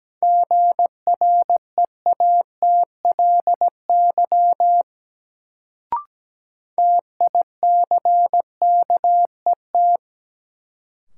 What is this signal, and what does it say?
Morse code, heard as a keyed tone, about 17 wpm, 700 Hz. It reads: GREATLY TICKET
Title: Sentences from Top 1400 Words - New Word Drill Alt - 17wpm